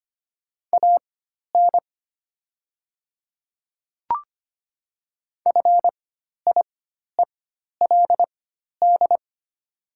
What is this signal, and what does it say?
Morse code, heard as a keyed tone, about 25 wpm, spaced as for 12 wpm, 700 Hz. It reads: AN FIELD
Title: 4x Character Spacing - Top 500 Words - 25wpm